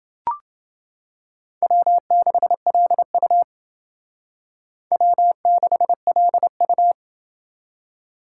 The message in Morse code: W6LU W6LU

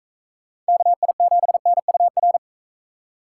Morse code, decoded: KI7NUR

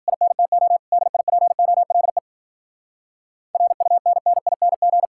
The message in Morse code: INTO DIPOLE RUNNING

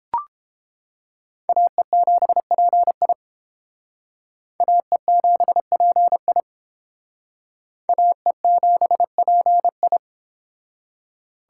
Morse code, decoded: AE7PI AE7PI AE7PI

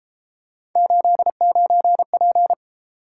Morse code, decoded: 89P

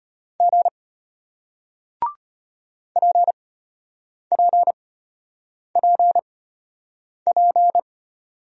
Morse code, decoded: G PPPP

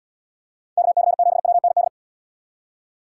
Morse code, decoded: KC7WND

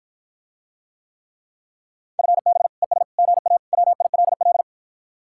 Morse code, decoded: FB ES GA WILL

